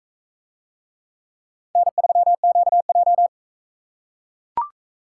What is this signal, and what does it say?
Morse code, heard as a keyed tone, about 40 wpm, 700 Hz. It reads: N3QJ